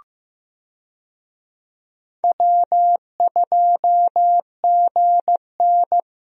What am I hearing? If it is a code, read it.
Morse code, heard as a keyed tone, about 15 wpm, 700 Hz. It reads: W2GN
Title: USA Call Signs - 4 Characters - Repeated 2x - 15wpm